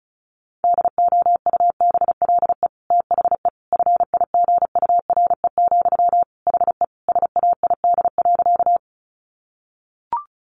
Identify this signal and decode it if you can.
Morse code, heard as a keyed tone, about 34 wpm, 700 Hz. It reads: DOUBLE THE FIGURE, HE SAID.
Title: Sentences from Top 1400 Words 35wpm